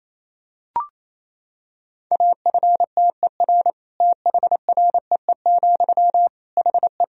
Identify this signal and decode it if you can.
Morse code, heard as a keyed tone, about 27 wpm, 700 Hz. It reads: AFTER THREE, HE
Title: Sentences from Top 200 Words and Abbreviations 28wpm